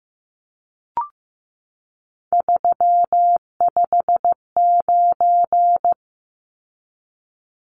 Morse code, heard as a keyed tone, about 15 wpm, 700 Hz. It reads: 359
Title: Numbers - 3-Digit 15wpm